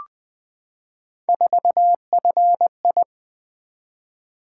Morse code, heard as a keyed tone, about 20 wpm, 700 Hz. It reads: 4FI
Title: Three Letter-Number Combinations - 20wpm